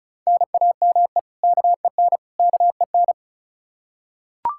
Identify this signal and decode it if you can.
Morse code, heard as a keyed tone, about 36 wpm, 700 Hz. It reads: NAME KEN KEN
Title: Practice QSO - Exchange 1 - Beginning - 35wpm